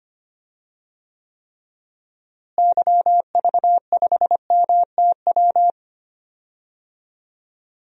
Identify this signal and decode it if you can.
Morse code, heard as a keyed tone, about 25 wpm, 700 Hz. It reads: YV5MTW
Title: DXCC-100 Call Signs - 6 Characters - Repeated 2x - 25wpm